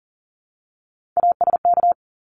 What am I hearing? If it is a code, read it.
Morse code, heard as a keyed tone, about 40 wpm, 700 Hz. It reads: ASK
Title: Top 300 Words Repeated 40wpm